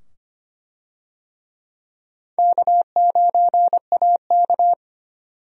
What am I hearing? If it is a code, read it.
Morse code, heard as a keyed tone, about 25 wpm, 700 Hz. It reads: K9AK
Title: USA Call Signs - 4 Characters - 25wpm